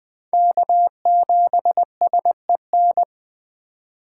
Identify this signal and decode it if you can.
Morse code, heard as a keyed tone, about 20 wpm, 700 Hz. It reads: K7SEN
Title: 2023 POTA Hunter Call Signs - 20wpm